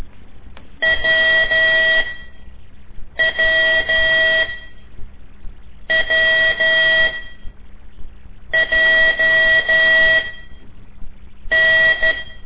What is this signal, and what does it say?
Morse code, heard as a keyed tone, about 12 wpm, spaced as for 5 wpm, 1950 Hz. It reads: WWWJN
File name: ik1weg2 / CORSI CW / lez5-1